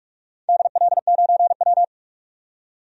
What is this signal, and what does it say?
Morse code, heard as a keyed tone, about 43 wpm, 700 Hz. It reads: DL9W
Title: DXCC-100 Call Signs - 4 Characters - 45wpm